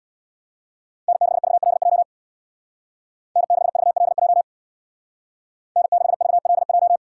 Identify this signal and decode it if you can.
Morse code, heard as a keyed tone, about 65 wpm, 700 Hz. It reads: N6VLJ N6VLJ N6VLJ